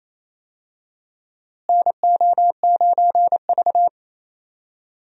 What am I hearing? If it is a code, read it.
Morse code, heard as a keyed tone, about 28 wpm, 700 Hz. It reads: NO9V
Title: USA Call Signs - 4 Characters - Repeated 2x - 28wpm